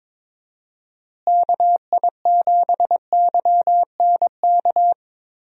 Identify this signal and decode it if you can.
Morse code, heard as a keyed tone, about 21 wpm, 700 Hz. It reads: KI7YNK
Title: USA Call Signs - 6 Characters - Repeated 3x - 22wpm